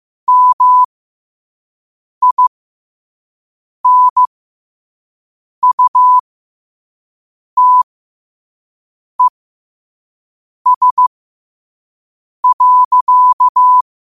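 Morse code, heard as a keyed tone, about 15 wpm, spaced as for 5 wpm, 1000 Hz. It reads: MINUTES.
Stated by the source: Morse quote-of-the-day practice at 5 wpm.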